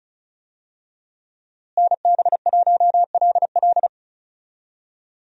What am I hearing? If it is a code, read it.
Morse code, heard as a keyed tone, about 34 wpm, 700 Hz. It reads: NB1LL
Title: USA Call Signs - 5 Characters - Repeated 3x - 35wpm